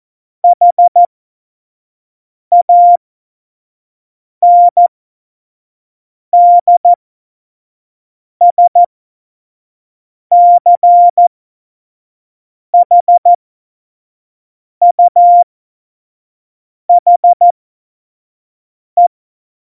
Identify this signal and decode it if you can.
Morse code, heard as a keyed tone, about 14 wpm, spaced as for 5 wpm, 700 Hz. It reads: HANDSCHUHE
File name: Morsekurs / Witze / Witz 28. Januar 2020, 25 F70